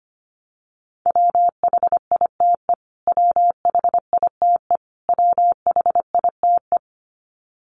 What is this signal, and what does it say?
Morse code, heard as a keyed tone, about 25 wpm, 700 Hz. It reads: WHITE WHITE WHITE
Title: Top 500 Words Repeated 25wpm